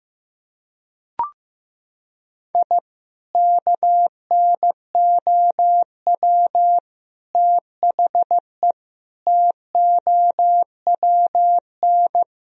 Morse code, heard as a keyed tone, about 15 wpm, 700 Hz. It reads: I KNOW THE TOWN
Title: Sentences from Top 1000 Words - Review Part 2 - 15wpm